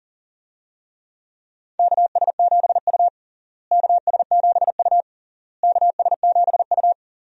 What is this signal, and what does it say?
Morse code, heard as a keyed tone, about 40 wpm, 700 Hz. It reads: KS7U KS7U KS7U
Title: USA Call Signs - 4 Characters - Repeated 3x - 40wpm